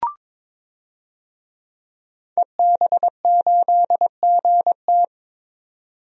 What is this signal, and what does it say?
Morse code, heard as a keyed tone, about 22 wpm, 700 Hz. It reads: EB8GT
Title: DXCC-100 Call Signs - 5 Characters - 22wpm